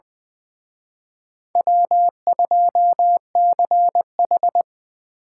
Morse code, heard as a keyed tone, about 23 wpm, 700 Hz. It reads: W2CH
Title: Speed Racing - USA 1x2 Call Signs 20wpm